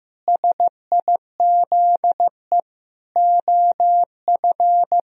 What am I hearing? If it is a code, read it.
Morse code, heard as a keyed tone, about 15 wpm, 700 Hz. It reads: SIZE OF
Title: Sentences from Top 1000 Words - Review Part 2 - 15wpm